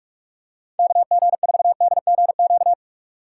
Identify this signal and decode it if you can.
Morse code, heard as a keyed tone, about 44 wpm, 700 Hz. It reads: KG4DGQ